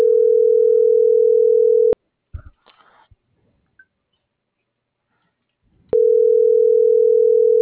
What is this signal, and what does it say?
Ringback tone (North America), 440 + 480 Hz.